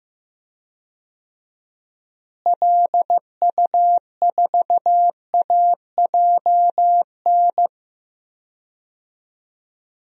Morse code, heard as a keyed tone, about 15 wpm, 700 Hz. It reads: LU4AJN
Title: DXCC-100 Call Signs - 6 Characters - Repeated 2x - 15wpm